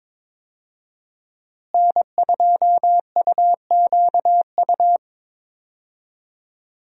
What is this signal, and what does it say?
Morse code, heard as a keyed tone, about 22 wpm, 700 Hz. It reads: N2UQU
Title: USA Call Signs - 5 Characters - Repeated 2x - 22wpm